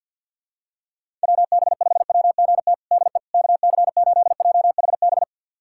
Morse code, heard as a keyed tone, about 46 wpm, 690 Hz. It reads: WB5WGT DE KC8JHB
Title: USA Callsign DE Callsign 50wpm